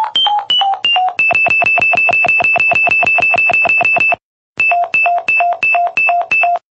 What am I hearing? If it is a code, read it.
There is no code.